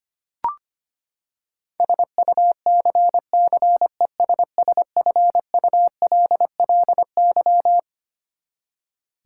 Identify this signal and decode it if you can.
Morse code, heard as a keyed tone, about 25 wpm, 700 Hz. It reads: SUCCESSFULLY
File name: Twelve Letter Words - 25wpm